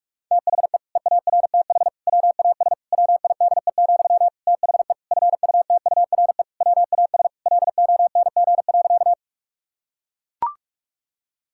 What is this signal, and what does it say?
Morse code, heard as a keyed tone, about 41 wpm, 700 Hz. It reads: THE EARTH WAS WIDE, THE FUTURE WAS LONG.
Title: Sentences from Top 1000 Words - Review Part 6 - 45wpm